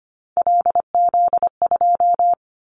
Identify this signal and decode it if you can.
Morse code, heard as a keyed tone, about 25 wpm, 700 Hz. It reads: LZ2
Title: Callsign Prefixes for Top 50 DXCC - 25wpm